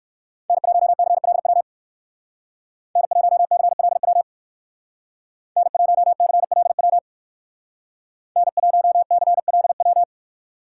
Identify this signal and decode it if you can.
Morse code, heard as a keyed tone, about 58 wpm, 700 Hz. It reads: N1CLW N1CLW N1CLW N1CLW